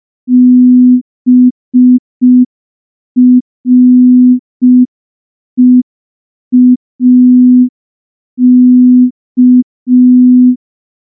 Morse code, heard as a keyed tone, about 5 wpm, 250 Hz. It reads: BREAK
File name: morse / morse-training-05WPM